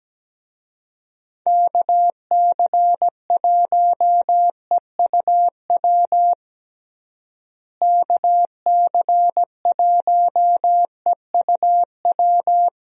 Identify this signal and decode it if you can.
Morse code, heard as a keyed tone, about 17 wpm, 700 Hz. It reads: KC1EUW KC1EUW
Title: USA Call Signs - 6 Characters - Repeated 3x - 17wpm